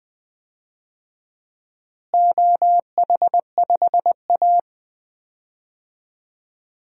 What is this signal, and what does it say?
Morse code, heard as a keyed tone, about 20 wpm, 700 Hz. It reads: OH5A
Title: DXCC-100 Call Signs - 4 Characters - Repeated 3x - 20wpm